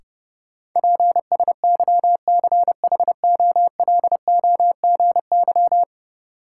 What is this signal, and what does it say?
Morse code, heard as a keyed tone, about 30 wpm, 700 Hz. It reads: PSYCHOLOGY
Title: Sentences from Top 1400 Words - New Word Drill Alt - 30wpm